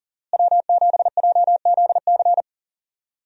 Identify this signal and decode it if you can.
Morse code, heard as a keyed tone, about 40 wpm, 700 Hz. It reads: W7JZC
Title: USA Call Signs - 5 Characters - Repeated 2x - 40wpm